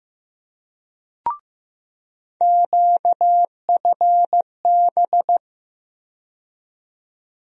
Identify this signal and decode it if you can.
Morse code, heard as a keyed tone, about 15 wpm, 700 Hz. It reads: QFB
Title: USA Call Sign Postfixes - 3 Letters 15wpm